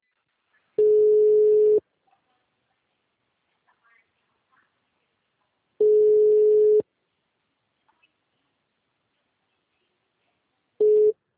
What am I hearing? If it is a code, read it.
Ringback tone (Europe, 425 Hz).
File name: out- -2011- - - .9210